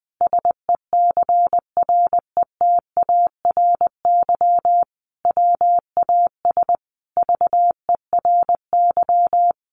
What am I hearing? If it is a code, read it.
Morse code, heard as a keyed tone, about 20 wpm, 700 Hz. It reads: SECRETARY WAS VERY
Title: Sentences from Top 1500 Words 20wpm